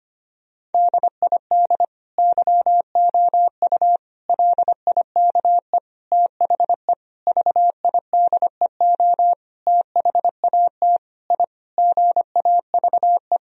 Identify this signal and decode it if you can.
Morse code, heard as a keyed tone, about 25 wpm, 700 Hz. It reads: DID YOU LIKE THE VIDEO THAT I GAVE
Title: Sentences from Top 1000 Words - Review Part 4 - 25wpm